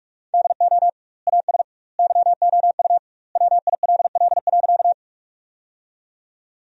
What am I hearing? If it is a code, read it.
Morse code, heard as a keyed tone, about 44 wpm, 700 Hz. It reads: DO AS YOU WILL.
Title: 2x Word-Spacing - Sentences from Top 100 Words - 45wpm